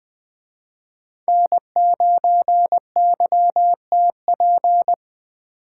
Morse code, heard as a keyed tone, about 21 wpm, 700 Hz. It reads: N9YTP